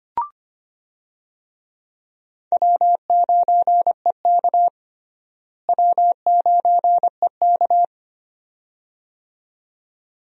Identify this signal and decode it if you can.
Morse code, heard as a keyed tone, about 25 wpm, 700 Hz. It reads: W9EK W9EK
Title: USA Call Signs - 4 Characters - Repeated 2x - 25wpm